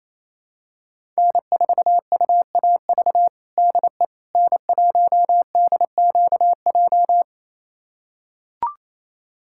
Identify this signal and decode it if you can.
Morse code, heard as a keyed tone, about 28 wpm, 700 Hz. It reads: N4UAV DE N1DQJ
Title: USA Callsign DE Callsign - 5 Characters - 28wpm